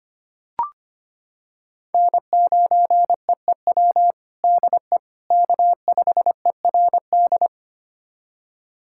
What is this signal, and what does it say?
Morse code, heard as a keyed tone, about 25 wpm, 700 Hz. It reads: N9EEW DE K5ERD
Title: USA Callsign DE Callsign - 5 Characters - 25wpm